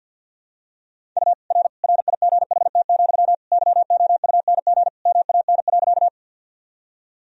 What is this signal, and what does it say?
Morse code, heard as a keyed tone, about 52 wpm, 700 Hz. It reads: U R RIGHT, YOUNG MAN.
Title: Sentences from Top 400 Words and Abbreviations 50wpm